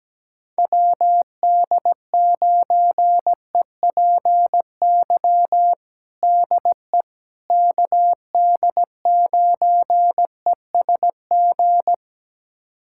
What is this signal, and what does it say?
Morse code, heard as a keyed tone, about 17 wpm, 700 Hz. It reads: WD9EPY DE KD9ESG
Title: USA Callsign DE Callsign 17wpm